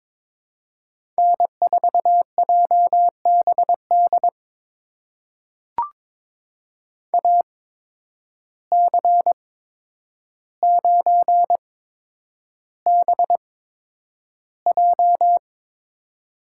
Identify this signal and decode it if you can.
Morse code, heard as a keyed tone, about 22 wpm, spaced as for 9 wpm, 700 Hz. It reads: N4JBD AC9BJ
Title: USA Call Signs - 5 Characters - 8x Character Spacing - 22wpm